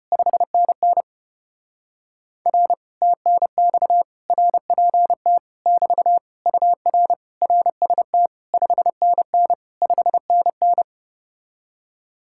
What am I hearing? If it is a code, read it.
Morse code, heard as a keyed tone, about 33 wpm, 700 Hz. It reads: 5NN R TNX RPT <BT> UR RST 5NN 5NN